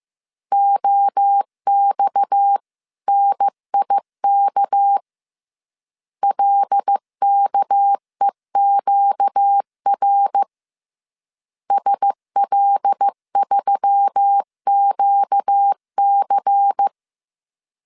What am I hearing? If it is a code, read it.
Morse code, heard as a keyed tone, about 15 wpm, 800 Hz. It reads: OXNIK LKEQR SL3QC
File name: CW / DET / 1 6 1500C S C 14wpm 2951